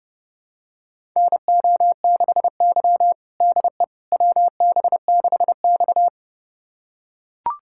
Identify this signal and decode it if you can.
Morse code, heard as a keyed tone, about 32 wpm, 700 Hz. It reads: NO6Y DE WB6X
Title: USA Callsign DE Callsign - 4 Characters - 30wpm